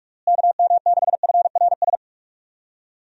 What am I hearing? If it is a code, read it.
Morse code, heard as a keyed tone, about 46 wpm, 700 Hz. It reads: KM6FRS
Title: USA Call Signs - 6 Characters - Repeated 3x - 45wpm